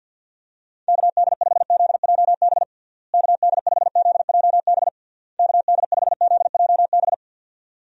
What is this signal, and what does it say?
Morse code, heard as a keyed tone, about 46 wpm, 700 Hz. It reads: KD5ZJB KD5ZJB KD5ZJB
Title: USA Call Signs - 6 Characters - Repeated 3x - 50wpm